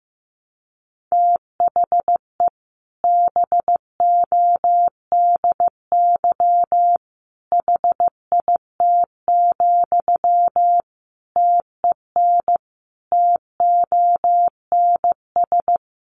Morse code, heard as a keyed tone, about 15 wpm, 700 Hz. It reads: THE BODY HIT, TEN TONS
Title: A Sound Of Thunder 15wpm